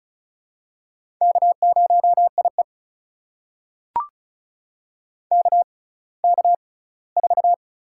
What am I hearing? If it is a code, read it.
Morse code, heard as a keyed tone, about 36 wpm, spaced as for 20 wpm, 700 Hz. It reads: K0IE KK4
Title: USA Call Signs - 6x Character Spacing - 35wpm